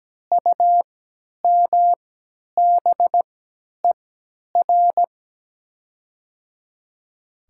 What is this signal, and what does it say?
Morse code, heard as a keyed tone, about 17 wpm, spaced as for 10 wpm, 700 Hz. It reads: UMBER
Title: 3x Character Spacing - Top 100 Words - 17wpm